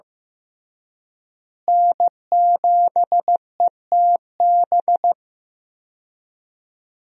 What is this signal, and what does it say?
Morse code, heard as a keyed tone, about 16 wpm, 700 Hz. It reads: N7ETB
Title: USA Call Signs - 5 Characters - Repeated 2x - 15wpm